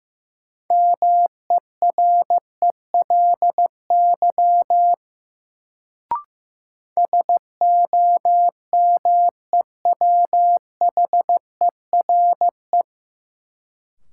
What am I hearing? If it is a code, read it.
Morse code, heard as a keyed tone, about 15 wpm, 700 Hz. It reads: MERELY SOMEWHERE
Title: Sentences from Top 1400 Words - New Word Drill Alt - 15wpm